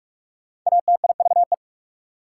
Morse code, heard as a keyed tone, about 47 wpm, 690 Hz. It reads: ATIVE
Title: Prefixes and Suffixes 45wpm